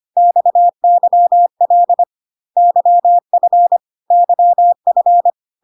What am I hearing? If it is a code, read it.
Morse code, heard as a keyed tone, about 25 wpm, 700 Hz. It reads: XYL YF YF
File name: CW abbreviations alphabetical repeated 3 times 25 25wpm